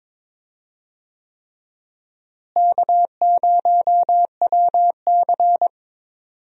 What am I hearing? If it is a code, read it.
Morse code, heard as a keyed tone, about 22 wpm, 700 Hz. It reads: K0WC
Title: USA Call Signs - 4 Characters - Repeated 3x - 22wpm